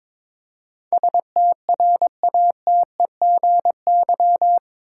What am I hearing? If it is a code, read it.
Morse code, heard as a keyed tone, about 21 wpm, 700 Hz. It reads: STRATEGY
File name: practice / 22wpm / Eight Letter Words - 22wpm